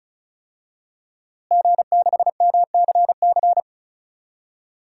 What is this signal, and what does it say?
Morse code, heard as a keyed tone, about 36 wpm, 700 Hz. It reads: G6MCC
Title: DXCC-100 Call Signs - 5 Characters - Repeated 2x - 35wpm